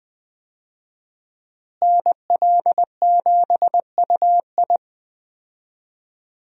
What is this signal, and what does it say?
Morse code, heard as a keyed tone, about 20 wpm, 700 Hz. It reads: NL7UI
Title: USA Call Signs - 5 Characters - Repeated 2x - 20wpm